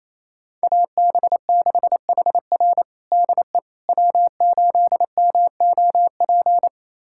Morse code, heard as a keyed tone, about 28 wpm, 690 Hz. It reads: AB6HR DE W8MOP
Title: USA Callsign DE Callsign - 5 Characters - 28wpm